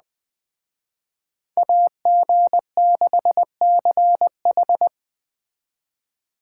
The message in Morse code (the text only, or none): AG6CH